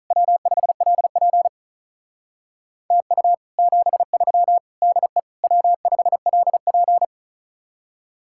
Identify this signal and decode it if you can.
Morse code, heard as a keyed tone, about 47 wpm, 700 Hz. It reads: W5LP TU 73 DE W5LP